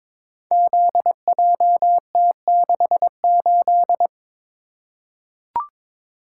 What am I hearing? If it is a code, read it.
Morse code, heard as a keyed tone, about 22 wpm, 700 Hz. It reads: ZJT68
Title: Code Groups - 5 Letter:Numbers - 22wpm